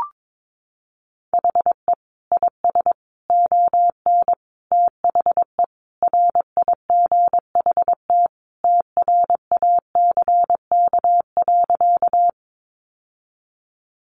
Morse code, heard as a keyed tone, about 22 wpm, 700 Hz. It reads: HE IS ON THE RIGHT TRACK.
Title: Sentences from Top 1100 Words 22wpm